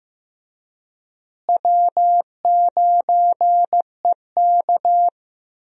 Morse code, heard as a keyed tone, about 16 wpm, 700 Hz. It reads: W9EK
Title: USA Call Signs - 4 Characters - Repeated 2x - 15wpm